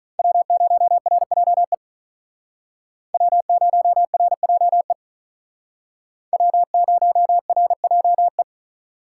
Morse code, heard as a keyed tone, about 42 wpm, 700 Hz. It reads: W0RJE W0RJE W0RJE
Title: Speed Racing - USA Call Signs 35wpm